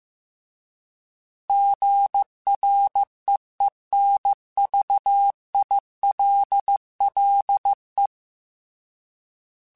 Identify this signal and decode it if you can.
Morse code, heard as a keyed tone, about 15 wpm, 800 Hz. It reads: GREENVILLE
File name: fromSMs / FromSMs mp3s / Cities@13